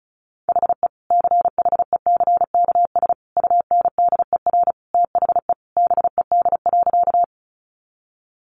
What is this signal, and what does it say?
Morse code, heard as a keyed tone, about 34 wpm, 700 Hz. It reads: HE CHECKS UNDER THE BED.
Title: Easy Phrases 35wpm